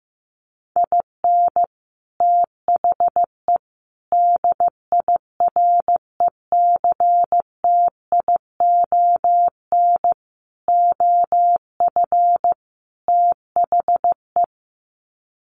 Morse code, heard as a keyed tone, about 15 wpm, 700 Hz. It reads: IN THE DIRECTION OF THE
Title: Sets of 5 Words 15wpm